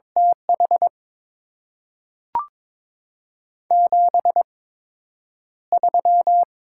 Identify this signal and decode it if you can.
Morse code, heard as a keyed tone, about 23 wpm, spaced as for 9 wpm, 700 Hz. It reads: TH 73
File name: Top 20 QSO Elements - 8x Character Spacing - 22wpm